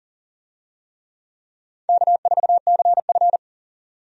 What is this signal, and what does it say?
Morse code, heard as a keyed tone, about 40 wpm, 700 Hz. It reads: K4CF
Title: USA Call Signs - 4 Characters - 40wpm